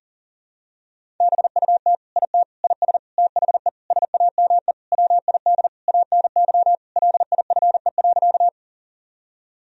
Morse code, heard as a keyed tone, about 40 wpm, 700 Hz. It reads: BUT IT IS THE SAME WID ANY LIFE.
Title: Sentences from Top 300 Words and Abbreviations 40wpm